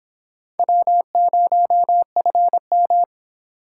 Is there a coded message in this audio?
Yes: Morse code from 0.6 to 3.0 s, heard as a keyed tone, about 26 wpm, 700 Hz. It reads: W0FM